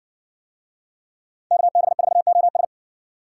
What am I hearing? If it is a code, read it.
Morse code, heard as a keyed tone, about 60 wpm, 700 Hz. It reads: KB4OS